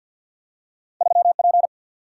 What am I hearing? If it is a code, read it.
Morse code, heard as a keyed tone, about 53 wpm, 700 Hz. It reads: 3 P
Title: Single-Letter Single-Number 50wpm